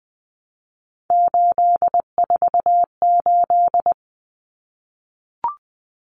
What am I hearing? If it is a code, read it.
Morse code, heard as a keyed tone, about 20 wpm, 700 Hz. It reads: 848
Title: Common Cut Numbers - 3-Digit 20wpm